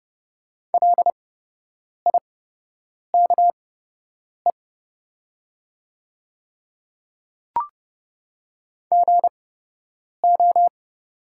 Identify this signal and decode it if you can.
Morse code, heard as a keyed tone, about 30 wpm, spaced as for 8 wpm, 700 Hz. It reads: LIKE GO
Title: 8x Character Spacing - Top 100 Words - 30wpm